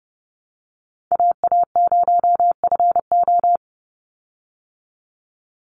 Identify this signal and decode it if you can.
Morse code, heard as a keyed tone, about 30 wpm, 700 Hz. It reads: AA0FO
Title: Call Signs Repeated 3x - 30wpm